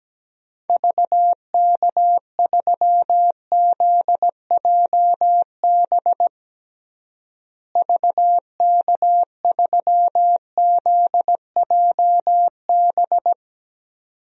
Morse code, heard as a keyed tone, about 17 wpm, 700 Hz. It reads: VK3ZJB VK3ZJB
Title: DXCC-100 Call Signs - 6 Characters - Repeated 2x - 17wpm